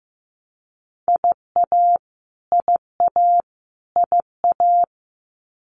Morse code, heard as a keyed tone, about 15 wpm, 700 Hz. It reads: IA IA IA
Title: US States Repeated 3x 15wpm